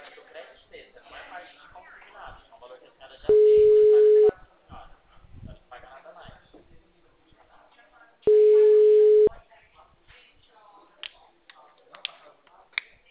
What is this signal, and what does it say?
Ringback tone (Europe, 425 Hz), with speech.